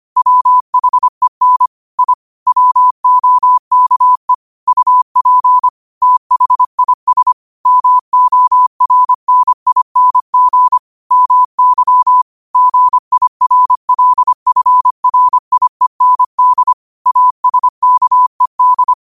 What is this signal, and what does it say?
Morse code, heard as a keyed tone, about 25 wpm, 1000 Hz. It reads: WHEN I WOKE UP THIS MORNING MY GIRLFRIEND ASKED